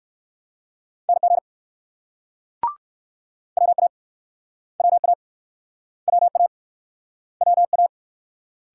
Morse code, heard as a keyed tone, about 56 wpm, 700 Hz. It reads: NG WA WA WA WA